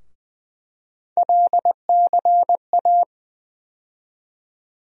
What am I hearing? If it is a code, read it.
Morse code, heard as a keyed tone, about 20 wpm, 700 Hz. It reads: LCA
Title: USA Call Sign Postfixes - 3 Letters 20wpm